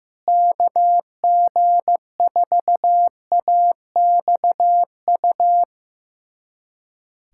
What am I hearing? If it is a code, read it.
Morse code, heard as a keyed tone, about 14 wpm, 700 Hz. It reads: KG4AXU
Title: USA Callsign DE Callsign - 6 Characters - 15wpm